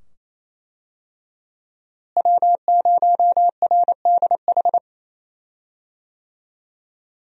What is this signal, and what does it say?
Morse code, heard as a keyed tone, about 27 wpm, 700 Hz. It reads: W0RDH
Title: USA Call Signs - 5 Characters - Repeated 2x - 28wpm